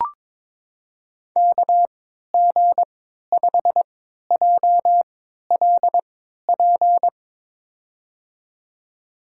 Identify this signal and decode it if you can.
Morse code, heard as a keyed tone, about 22 wpm, spaced as for 13 wpm, 700 Hz. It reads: KG5JLP
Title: USA Call Signs - 3x Character Spacing - 22wpm